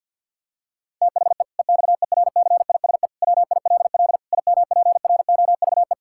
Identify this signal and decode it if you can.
Morse code, heard as a keyed tone, about 50 wpm, 700 Hz. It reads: THE EXERCISE WILL IMPROVE
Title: Sentences from Top 1000 Words - Review Part 6 - 50wpm